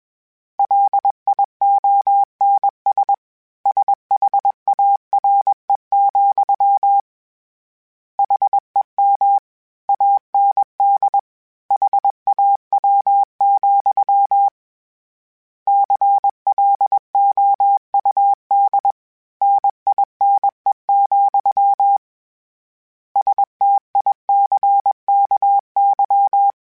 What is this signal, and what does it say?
Morse code, heard as a keyed tone, about 21 wpm, 800 Hz. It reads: LIONS SHARE, HEM AND HAW, CLOUD NINE, STICKY
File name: Misc / Misc-mp3s / Word Pairs @20